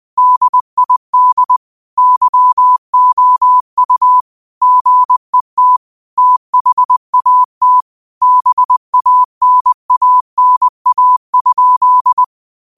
Morse code, heard as a keyed tone, about 20 wpm, 1000 Hz. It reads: DID YOU GET THAT BANANA?
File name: qotdfiles / 20wpm